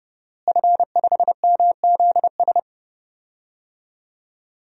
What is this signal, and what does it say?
Morse code, heard as a keyed tone, about 30 wpm, 700 Hz. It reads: F5MZS
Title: DXCC-100 Call Signs - 5 Characters - Repeated 3x - 30wpm